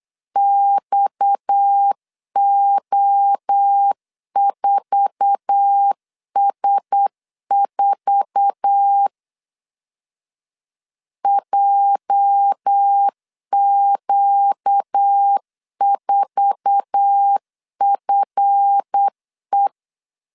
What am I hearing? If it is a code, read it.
Morse code, heard as a keyed tone, about 8 wpm, 800 Hz. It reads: XO4S4 JQ4FE
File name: LEZIONI / 8 YJ45 / 8 YJ4 8wpm 3246